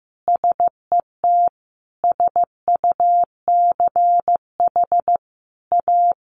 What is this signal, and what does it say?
Morse code, heard as a keyed tone, about 14 wpm, 700 Hz. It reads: SET SUCH A
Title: Sentences from Top 1300 Words 15wpm